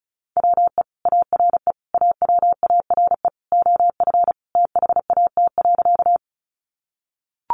Morse code, heard as a keyed tone, about 34 wpm, 700 Hz. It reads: WE ARE AWARE OF THAT.
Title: Sentences from Top 1200 Words 35wpm